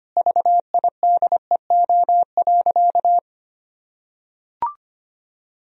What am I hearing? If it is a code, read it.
Morse code, heard as a keyed tone, about 25 wpm, 700 Hz. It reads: VIDEO.
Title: Sentences from Top 1000 Words - Review Part 4 - 25wpm